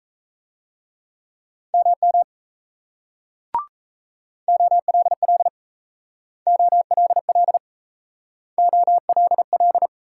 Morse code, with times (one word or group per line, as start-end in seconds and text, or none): 1.7-2.2 MM
4.5-5.5 OLL
6.5-7.6 OLL
8.6-9.9 OLL